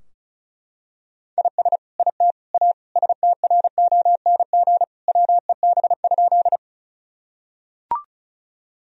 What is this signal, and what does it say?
Morse code, heard as a keyed tone, about 34 wpm, 700 Hz. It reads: IS IT A STRONG WEB?
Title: Sentences from Top 1000 Words - Review Part 4 - 35wpm